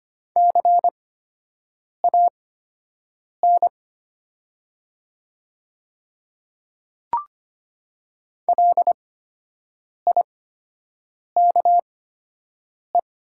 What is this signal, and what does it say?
Morse code, heard as a keyed tone, about 25 wpm, spaced as for 7 wpm, 700 Hz. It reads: CAN LIKE
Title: 8x Character Spacing - Top 100 Words - 25wpm